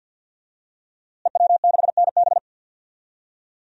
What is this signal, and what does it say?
Morse code, heard as a keyed tone, about 47 wpm, 700 Hz. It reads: EW6NB
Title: DXCC-100 Call Signs - 5 Characters - Repeated 3x - 50wpm